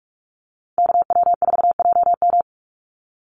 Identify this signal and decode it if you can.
Morse code, heard as a keyed tone, about 44 wpm, 700 Hz. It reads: KW4JM
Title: NA SOTA Chaser Callsigns 45wpm